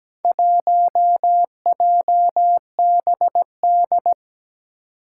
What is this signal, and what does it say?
Morse code, heard as a keyed tone, about 17 wpm, 700 Hz. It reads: 1JBD
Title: Four Challenging Letter-Number Combinations 17wpm